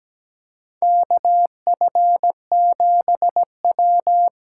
Morse code, heard as a keyed tone, about 17 wpm, 700 Hz. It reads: KF7W
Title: USA Call Signs - 6 Characters - 8x Character Spacing - 17wpm